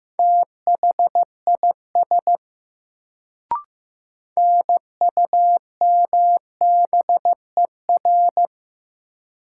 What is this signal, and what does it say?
Morse code, heard as a keyed tone, about 15 wpm, 700 Hz. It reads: THIS NUMBER
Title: Sets of 2 Words - New Word Drill - 15wpm